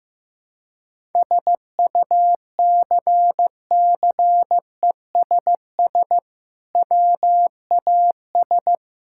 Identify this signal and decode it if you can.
Morse code, heard as a keyed tone, about 15 wpm, 700 Hz. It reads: SUCCESS WAS
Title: Sentences from Top 1000 Words - Review Part 6 - 15wpm